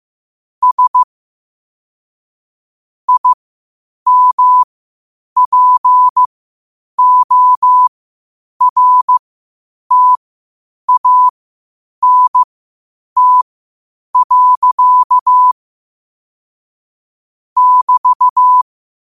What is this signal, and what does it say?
Morse code, heard as a keyed tone, about 15 wpm, spaced as for 8 wpm, 1000 Hz. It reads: S IMPORTANT. <BT>
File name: qotdfiles / 8wpm